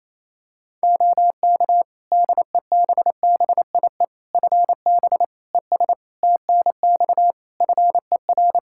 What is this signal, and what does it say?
Morse code, heard as a keyed tone, about 28 wpm, 700 Hz. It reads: OK DEBBIE FB ES TNX FER
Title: Practice QSO - Exchange 2 - Info - 28wpm